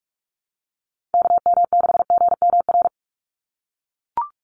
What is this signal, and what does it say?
Morse code, heard as a keyed tone, about 44 wpm, 700 Hz. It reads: KM6GMR
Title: USA Call Signs 45wpm